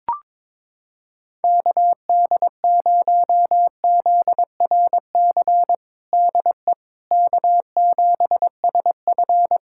Morse code, heard as a keyed tone, about 21 wpm, 700 Hz. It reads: KD0ZRC DE K7SF
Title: Practice QSO - Exchange 4 - Ending - 22wpm